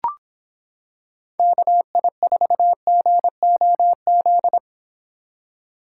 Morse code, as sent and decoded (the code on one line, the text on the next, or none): -.- .. ....- --. --- --..
KI4GOZ